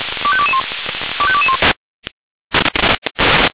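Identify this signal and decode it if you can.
Five-tone selective call (ZVEI), heard as quick runs of five tones, tones of about 70 ms. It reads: 25221 25221
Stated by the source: Five-tone ZVEI selective call received on SDR.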